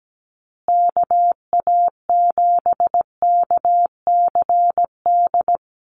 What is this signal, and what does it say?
Morse code, heard as a keyed tone, about 17 wpm, 700 Hz. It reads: KA7KCD
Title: USA Call Signs Repeated 3x 17wpm